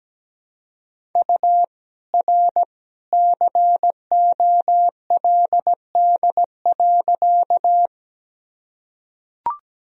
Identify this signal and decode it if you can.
Morse code, heard as a keyed tone, about 17 wpm, 700 Hz. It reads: U R COLD.
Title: Sentences from Top 400 Words and Abbreviations 17wpm